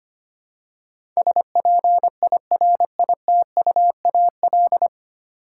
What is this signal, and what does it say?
Morse code, heard as a keyed tone, about 25 wpm, 700 Hz. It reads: SPIRITUAL
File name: 9 Letter Words - First 3 Letters Given - 25wpm